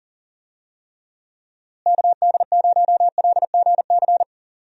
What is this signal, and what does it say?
Morse code, heard as a keyed tone, about 40 wpm, 700 Hz. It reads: KD0LGC
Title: USA Call Signs - 6 Characters - Repeated 2x - 40wpm